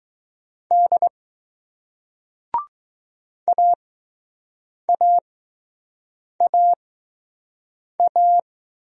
Morse code, heard as a keyed tone, about 20 wpm, spaced as for 6 wpm, 700 Hz. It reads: D AAAA